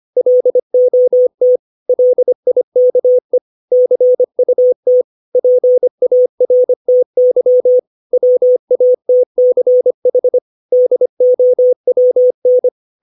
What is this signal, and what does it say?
Morse code, heard as a keyed tone, about 25 wpm, 500 Hz. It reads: LOT LIKE CUT PARTY WATCH DOWN
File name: 632 words / 632 Words 25WPM